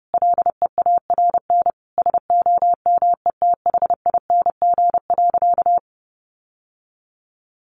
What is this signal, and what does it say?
Morse code, heard as a keyed tone, about 30 wpm, 700 Hz. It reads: LEARN SOMETHING.
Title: Sentences from Top 600 Words 30wpm